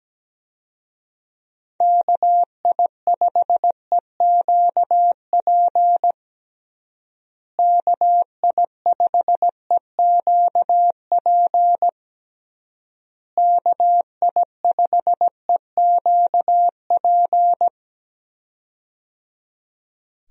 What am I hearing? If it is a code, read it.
Morse code, heard as a keyed tone, about 17 wpm, 700 Hz. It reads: KI5EQP KI5EQP KI5EQP
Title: USA Call Signs - 6 Characters - Repeated 3x - 17wpm